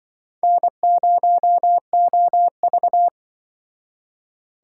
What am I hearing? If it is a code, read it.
Morse code, heard as a keyed tone, about 24 wpm, 700 Hz. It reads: N0OV
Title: Speed Racing - USA 1x2 Call Signs 20wpm